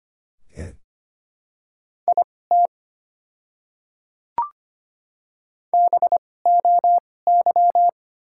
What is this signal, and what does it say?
Morse code, heard as a keyed tone, about 25 wpm, spaced as for 18 wpm, 690 Hz. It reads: IT BOY
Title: 2x Character Spacing - Top 300 Words - 25wpm